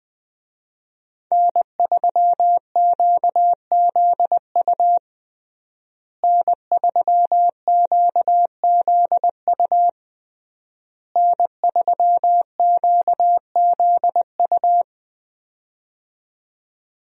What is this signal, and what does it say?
Morse code, heard as a keyed tone, about 20 wpm, 700 Hz. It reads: N3QZU N3QZU N3QZU
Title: USA Call Signs - 5 Characters - Repeated 3x - 20wpm